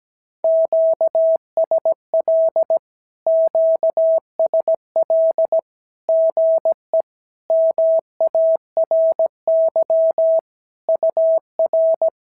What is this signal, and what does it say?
Morse code, heard as a keyed tone, about 17 wpm, 650 Hz. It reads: QSL QSL GE MARY UR
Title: SOTA QSO Practice 17wpm